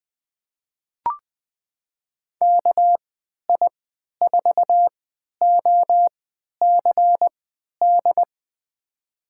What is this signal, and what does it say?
Morse code, heard as a keyed tone, about 20 wpm, spaced as for 11 wpm, 700 Hz. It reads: KI4OCD
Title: USA Call Signs - 3x Character Spacing - 20wpm